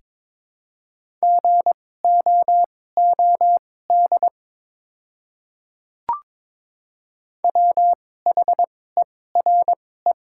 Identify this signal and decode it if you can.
Morse code, heard as a keyed tone, about 22 wpm, spaced as for 16 wpm, 700 Hz. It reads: GOOD WHERE
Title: 2x Character Spacing - Top 200 Words - 22wpm